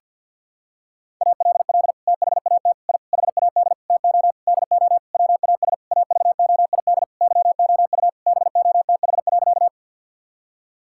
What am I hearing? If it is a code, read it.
Morse code, heard as a keyed tone, about 50 wpm, 700 Hz. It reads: ALL THAT I HAD TO DO WAS AVOID YOU BOTH.
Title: Sentences from Top 1000 Words - Review Part 6 - 50wpm